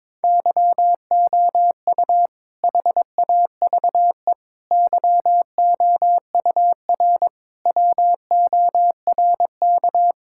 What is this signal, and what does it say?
Morse code, heard as a keyed tone, about 22 wpm, 700 Hz. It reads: YOU HAVE YOUR WORK
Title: Sentences from Top 1000 Words - Review Part 5 - 22wpm